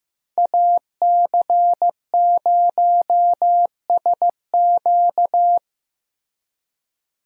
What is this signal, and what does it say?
Morse code, heard as a keyed tone, about 15 wpm, 700 Hz. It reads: AC0SQ
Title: USA Call Signs - 5 Characters - Repeated 3x - 15wpm